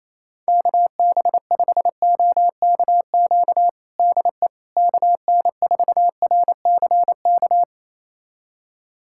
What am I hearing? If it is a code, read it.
Morse code, heard as a keyed tone, about 28 wpm, 700 Hz. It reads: KB5OKQ DE KN4RCK
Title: USA Callsign DE Callsign 28wpm